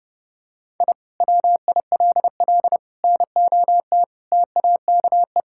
Morse code, heard as a keyed tone, about 30 wpm, 700 Hz. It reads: I WILL NOT TAKE
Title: Sentences from Top 1000 Words - Review Part 6 - 30wpm